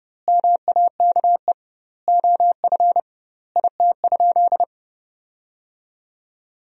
Morse code, heard as a keyed tone, about 29 wpm, 700 Hz. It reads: MAKE OF IT?
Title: 2x Word-Spacing - Sentences from Top 100 Words - 30wpm